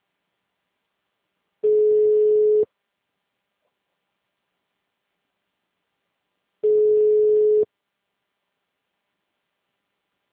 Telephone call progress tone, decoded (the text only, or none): ringback tone